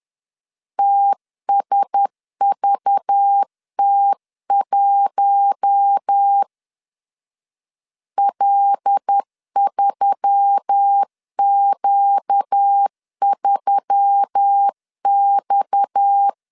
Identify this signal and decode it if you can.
Morse code, heard as a keyed tone, about 10 wpm, 800 Hz. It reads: TSVT1 L3Q3X
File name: CW / DET / 1 6 1500C S B 10wpm 4137